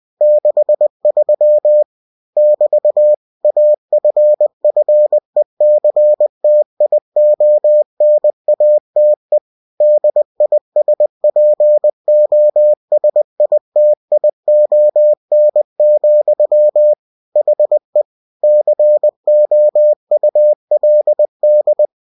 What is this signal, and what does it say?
Morse code, heard as a keyed tone, about 20 wpm, 600 Hz. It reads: 63 <BT> AFFECTIONATE DISPOSITION, HE COULD
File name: cw-books / winnie-the-pooh / winnie-20wpm-page-0063